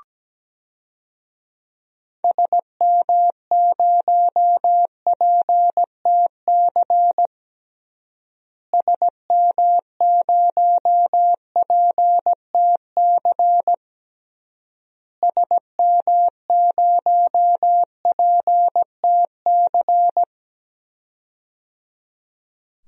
Morse code, heard as a keyed tone, about 17 wpm, 700 Hz. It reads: SM0PTC SM0PTC SM0PTC
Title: DXCC-100 Call Signs - 6 Characters - Repeated 3x - 17wpm